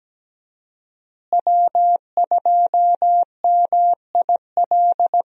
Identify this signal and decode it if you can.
Morse code, heard as a keyed tone, about 17 wpm, 700 Hz. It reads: W2MIL
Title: USA Call Signs - 5 Characters - Repeated 2x - 17wpm